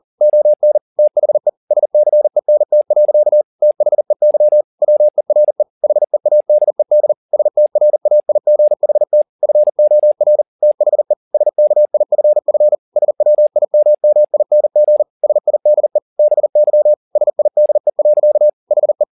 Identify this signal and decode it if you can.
Morse code, heard as a keyed tone, about 40 wpm, 600 Hz. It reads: ON THE SCENT. THEY WERE HEADED STRAIGHT FOR THE SKIFF SWIMMING SIDE BY SIDE. HE
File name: the old man and the sea 2 40wpm